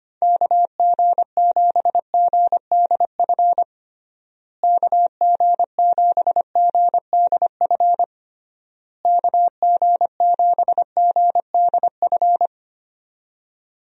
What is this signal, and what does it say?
Morse code, heard as a keyed tone, about 25 wpm, 700 Hz. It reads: KG7GDF KG7GDF KG7GDF
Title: USA Call Signs - 6 Characters - Repeated 3x - 25wpm